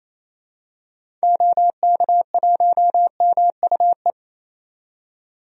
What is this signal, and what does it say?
Morse code, heard as a keyed tone, about 28 wpm, 700 Hz. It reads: OK1MUE
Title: DXCC-100 Call Signs - 6 Characters - 28wpm